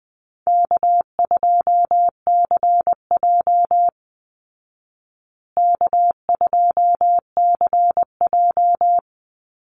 Morse code, heard as a keyed tone, about 20 wpm, 700 Hz. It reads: K2CJ K2CJ
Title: Easy Call Signs Repeated 3x - 20wpm